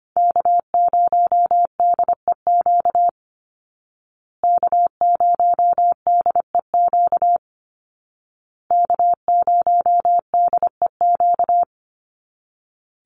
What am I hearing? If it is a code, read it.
Morse code, heard as a keyed tone, about 25 wpm, 700 Hz. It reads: K0DEQ K0DEQ K0DEQ
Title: Call Signs Repeated 2x - 25wpm